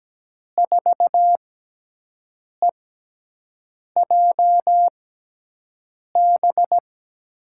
Morse code, heard as a keyed tone, about 17 wpm, spaced as for 6 wpm, 700 Hz. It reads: 4EJB
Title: USA Call Signs - 6 Characters - 6x Character Spacing - 17wpm